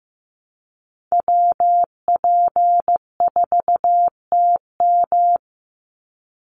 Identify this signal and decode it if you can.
Morse code, heard as a keyed tone, about 15 wpm, 700 Hz. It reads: WP4TM
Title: USA Call Signs 15wpm